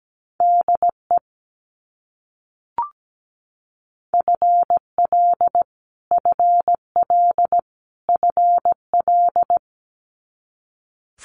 Morse code, heard as a keyed tone, about 17 wpm, 700 Hz. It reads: DE FL FL FL
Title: US States Repeated 3x 17wpm